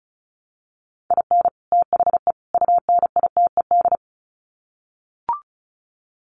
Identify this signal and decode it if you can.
Morse code, heard as a keyed tone, about 34 wpm, 700 Hz. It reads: IN THE UNITED
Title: Sets of 3 Words 35wpm